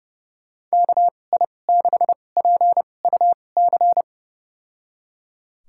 Morse code, heard as a keyed tone, about 30 wpm, spaced as for 22 wpm, 700 Hz. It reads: KI6PUC
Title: USA Call Signs - 6 Characters - 2x Character Spacing - 30wpm